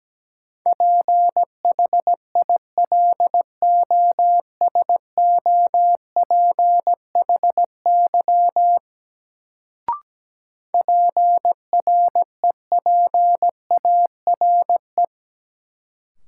Morse code, heard as a keyed tone, about 17 wpm, 700 Hz. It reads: PHILOSOPHY PREPARE
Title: Sentences from Top 900 Words - New Word Drill - 17wpm